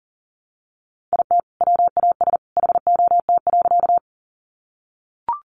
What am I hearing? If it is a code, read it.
Morse code, heard as a keyed tone, about 40 wpm, 700 Hz. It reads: IT WAS HOT.
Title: Sentences from Top 500 Words 40wpm